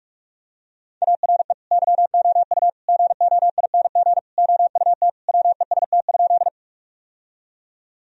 Morse code, heard as a keyed tone, about 46 wpm, 700 Hz. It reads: ARE YOU GOING OUT WEST?
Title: Sentences from Top 1000 Words - Review Part 2 - 45wpm